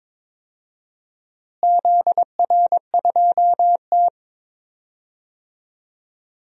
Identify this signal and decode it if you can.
Morse code, heard as a keyed tone, about 22 wpm, 700 Hz. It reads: ZR2T
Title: DXCC-100 Call Signs - 4 Characters - Repeated 3x - 22wpm